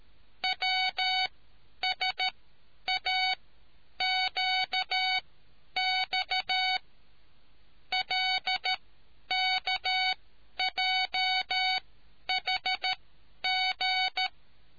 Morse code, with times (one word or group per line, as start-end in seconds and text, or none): 0.4-6.8 WSAQX
7.9-14.3 LKJHG